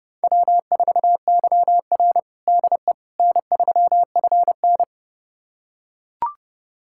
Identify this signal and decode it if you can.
Morse code, heard as a keyed tone, about 30 wpm, 700 Hz. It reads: W4YR DE N3FN
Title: USA Callsign DE Callsign - 4 Characters - 30wpm